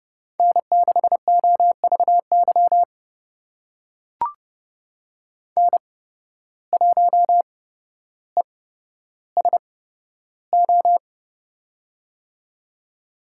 Morse code, heard as a keyed tone, about 29 wpm, spaced as for 13 wpm, 700 Hz. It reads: N6OVY N1ESO
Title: USA Call Signs - 5 Characters - 8x Character Spacing - 30wpm